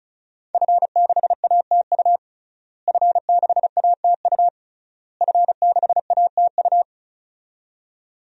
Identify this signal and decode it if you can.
Morse code, heard as a keyed tone, about 36 wpm, 700 Hz. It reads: F6ATU F6ATU F6ATU
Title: DXCC-100 Call Signs - 5 Characters - Repeated 3x - 35wpm